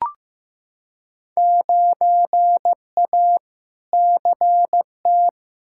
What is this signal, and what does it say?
Morse code, heard as a keyed tone, about 15 wpm, 700 Hz. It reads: 9A CT
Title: Field Day Exchanges Encore - 15wpm